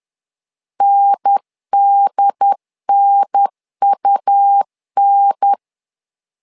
Morse code, heard as a keyed tone, about 10 wpm, 800 Hz. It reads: NDNUN
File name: LEZIONI / 2 UHDN / 2 UHDN 10wpm 1904